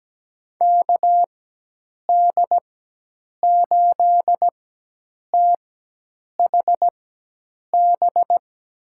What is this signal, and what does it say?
Morse code, heard as a keyed tone, about 17 wpm, spaced as for 8 wpm, 700 Hz. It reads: KD8THB
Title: USA Call Signs - 6 Characters - 4x Character Spacing - 17wpm